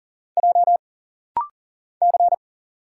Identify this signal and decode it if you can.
Morse code, heard as a keyed tone, about 40 wpm, 700 Hz. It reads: J C
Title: Single Letters - Rapid-Fire 40wpm